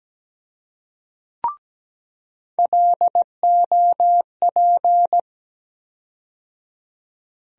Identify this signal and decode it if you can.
Morse code, heard as a keyed tone, about 17 wpm, 700 Hz. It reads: LOP
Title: USA Call Sign Postfixes - 3 Letters 17wpm